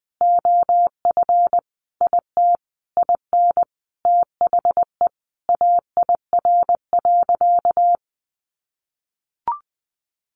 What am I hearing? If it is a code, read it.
Morse code, heard as a keyed tone, about 20 wpm, 700 Hz. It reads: OF IT IN THE AIR.
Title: Sentences from Top 900 Words 20wpm